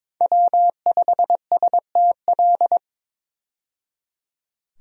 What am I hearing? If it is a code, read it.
Morse code, heard as a keyed tone, about 22 wpm, 700 Hz. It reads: W5STL
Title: USA Call Signs - 5 Characters - 22wpm